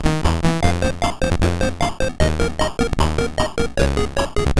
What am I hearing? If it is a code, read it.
There is no code.